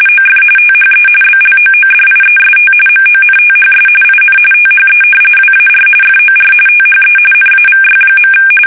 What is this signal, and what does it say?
RTTY, about 45.45 baud. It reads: OF THE BLOODIEST MONTHS IN THE WAR FOR THE NAVY.
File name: fsk850 / battleshipnj